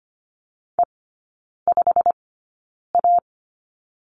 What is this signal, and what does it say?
Morse code, heard as a keyed tone, about 25 wpm, spaced as for 9 wpm, 700 Hz. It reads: E5A
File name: course / 25:10wpm / 093 Lesson 16e - Letter c - Three Characters 25wpm